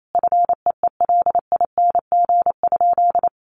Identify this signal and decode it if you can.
Morse code, heard as a keyed tone, about 28 wpm, 700 Hz. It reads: FEELING?
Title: Sentences from Top 1200 Words 28wpm